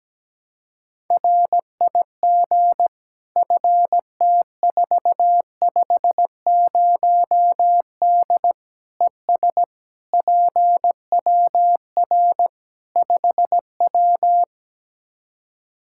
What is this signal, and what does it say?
Morse code, heard as a keyed tone, about 17 wpm, 700 Hz. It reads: RIG FT450D ES PWR 5W
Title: Practice QSO - Exchange 2 - Info - 17wpm